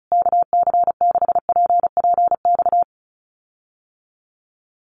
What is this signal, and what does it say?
Morse code, heard as a keyed tone, about 34 wpm, 700 Hz. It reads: KC6PPX
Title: USA Call Signs Repeated 3x 35wpm